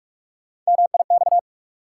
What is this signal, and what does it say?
Morse code, heard as a keyed tone, about 46 wpm, 700 Hz. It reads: MIX
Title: Sentences from Top 1000 Words - Review Part 4 - New Word Drill - 45wpm